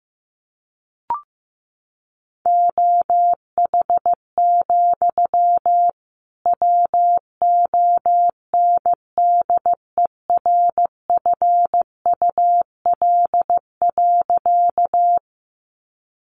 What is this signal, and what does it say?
Morse code, heard as a keyed tone, about 15 wpm, 700 Hz. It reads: OH, WONDERFUL.
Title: Sentences from Top 1100 Words 15wpm